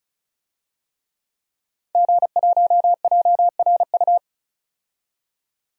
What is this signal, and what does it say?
Morse code, heard as a keyed tone, about 34 wpm, 700 Hz. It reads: G1JRU
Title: DXCC-100 Call Signs - 5 Characters - 35wpm